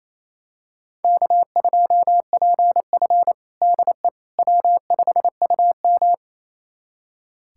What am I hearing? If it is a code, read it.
Morse code, heard as a keyed tone, about 28 wpm, 700 Hz. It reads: K2PF DE W5UM
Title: USA Callsign DE Callsign - 4 Characters - 28wpm